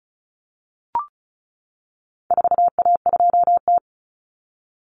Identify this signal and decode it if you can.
Morse code, heard as a keyed tone, about 34 wpm, 700 Hz. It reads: 4A2T
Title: Common Cut Numbers - 4-Digit 35wpm